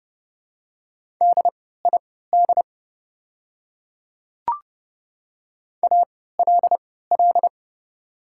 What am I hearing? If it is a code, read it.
Morse code, heard as a keyed tone, about 30 wpm, spaced as for 17 wpm, 700 Hz. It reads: DID ALL
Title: 3x Character Spacing - Top 100 Words - 30wpm